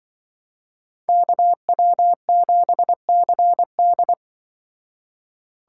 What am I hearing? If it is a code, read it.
Morse code, heard as a keyed tone, about 25 wpm, 700 Hz. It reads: KW7CD
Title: Speed Racing - USA 2x2 Call Signs 20wpm